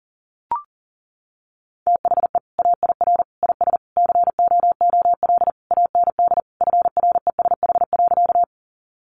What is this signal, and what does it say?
Morse code, heard as a keyed tone, about 40 wpm, 700 Hz. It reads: THE AIR IS COOL AND FRESH.
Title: Sentences from Top 1200 Words 40wpm